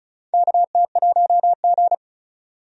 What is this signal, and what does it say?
Morse code, heard as a keyed tone, about 36 wpm, 700 Hz. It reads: KT1G
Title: USA Call Signs - 4 Characters - 35wpm